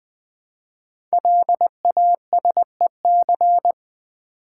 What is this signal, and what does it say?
Morse code, heard as a keyed tone, about 20 wpm, 700 Hz. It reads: LASEC
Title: Code Groups - 5 Letters - 20wpm